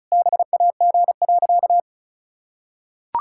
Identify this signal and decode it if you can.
Morse code, heard as a keyed tone, about 34 wpm, 700 Hz. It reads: BAG.
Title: Pangram Challenge v2 - 35wpm